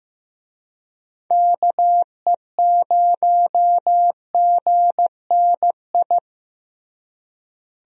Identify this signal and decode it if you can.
Morse code, heard as a keyed tone, about 15 wpm, 700 Hz. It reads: KE0GNI
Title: USA Call Signs - 6 Characters - Repeated 2x - 15wpm